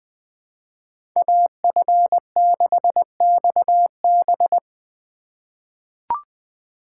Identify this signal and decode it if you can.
Morse code, heard as a keyed tone, about 20 wpm, 700 Hz. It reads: AF6XB
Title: USA Call Signs - 5 Characters - 4x Character Spacing - 20wpm